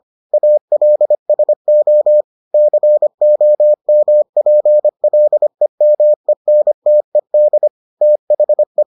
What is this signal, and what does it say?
Morse code, heard as a keyed tone, about 25 wpm, 600 Hz. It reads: ALSO COMPLEMENTED THE